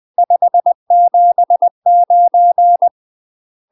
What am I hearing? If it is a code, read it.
Morse code, heard as a keyed tone, about 20 wpm, 700 Hz. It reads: 579
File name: UK MASTER QSOs TRAINING / 20WPM / UK MASTER QSOs 0060